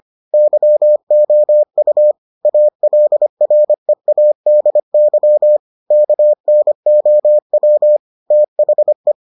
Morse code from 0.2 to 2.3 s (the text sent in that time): YOU